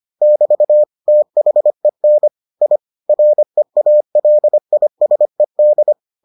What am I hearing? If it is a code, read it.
Morse code, heard as a keyed tone, about 25 wpm, 600 Hz. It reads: <BT> THEN I REALISED